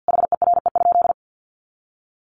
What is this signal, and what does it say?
Morse code, heard as a keyed tone, about 48 wpm, 700 Hz. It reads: HERE?
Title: Sentences from Top 300 Words 50wpm